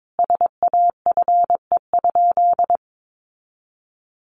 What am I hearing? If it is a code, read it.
Morse code, heard as a keyed tone, about 22 wpm, 700 Hz. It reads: SAFE?
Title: Sentences from Top 900 Words 22wpm